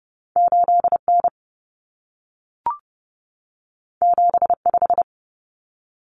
Morse code, heard as a keyed tone, about 30 wpm, 700 Hz. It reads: 8N 75
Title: Common Cut Numbers - 2-Digit 30wpm